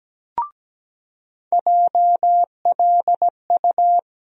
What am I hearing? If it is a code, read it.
Morse code, heard as a keyed tone, about 17 wpm, 700 Hz. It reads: JLU
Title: USA Call Sign Postfixes - 3 Letters 17wpm